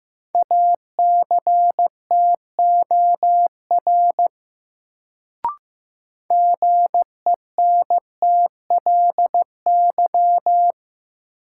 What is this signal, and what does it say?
Morse code, heard as a keyed tone, about 15 wpm, 700 Hz. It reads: ACTOR GENTLY
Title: Sentences from Top 1500 Words - New Word Drill - 15wpm